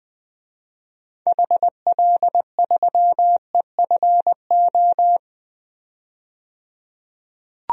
Morse code, heard as a keyed tone, about 20 wpm, 700 Hz. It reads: HL3EFO
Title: DXCC-100 Call Signs - 6 Characters - 20wpm